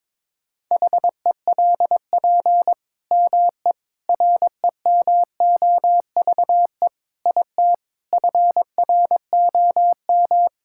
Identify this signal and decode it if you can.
Morse code, heard as a keyed tone, about 21 wpm, 700 Hz. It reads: HELP ME REMOVE IT FROM
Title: Sentences from Top 1000 Words - Review Part 5 - 22wpm